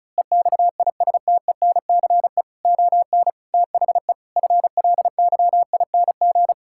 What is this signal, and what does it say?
Morse code, heard as a keyed tone, about 34 wpm, 700 Hz. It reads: EXISTENCE ON THE FLYING
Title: Pangram Challenge v2 - 35wpm